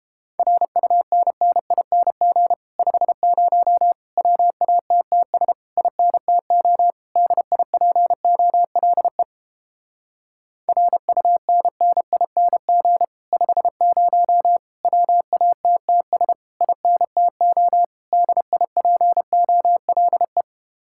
Morse code, heard as a keyed tone, about 31 wpm, 700 Hz. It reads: RUNNING 50 WATTS INTO DIPOLE RUNNING 50 WATTS INTO DIPOLE